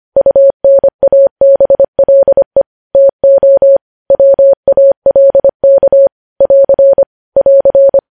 Morse code, heard as a keyed tone, about 25 wpm, 550 Hz. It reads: UNABLE TO WALK <AR> <AR>
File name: podcast / News headlines-1x-25x25